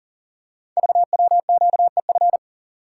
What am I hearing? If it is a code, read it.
Morse code, heard as a keyed tone, about 38 wpm, 690 Hz. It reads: VWQEF
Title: Code Groups - 5 Letters - 40wpm